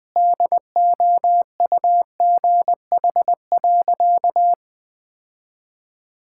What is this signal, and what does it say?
Morse code, heard as a keyed tone, about 20 wpm, 700 Hz. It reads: DOUGH.
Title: Easy Sentences with Hints - 20wpm